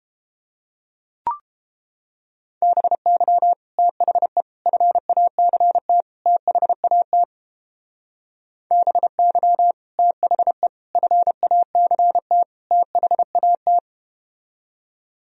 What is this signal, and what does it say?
Morse code, heard as a keyed tone, about 31 wpm, 700 Hz. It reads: BY THE FACT THAT BY THE FACT THAT